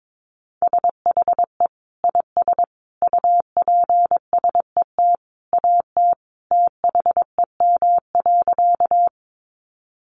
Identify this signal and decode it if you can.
Morse code, heard as a keyed tone, about 22 wpm, 700 Hz. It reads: SHE IS UPSET AT THEM.
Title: Sentences from Top 800 Words 22wpm